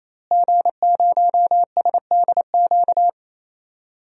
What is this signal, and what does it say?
Morse code, heard as a keyed tone, about 28 wpm, 700 Hz. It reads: G0SDQ
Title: DXCC-100 Call Signs - 5 Characters - Repeated 3x - 28wpm